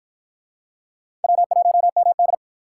Morse code, heard as a keyed tone, about 52 wpm, 700 Hz. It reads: W1MD